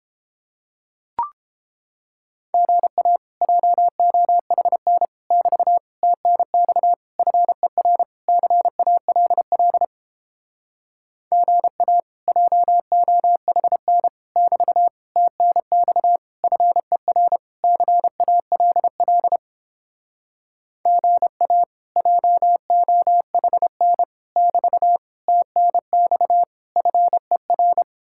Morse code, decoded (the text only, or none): GA JOHN <BT> TNX FER CALL GA JOHN <BT> TNX FER CALL GA JOHN <BT> TNX FER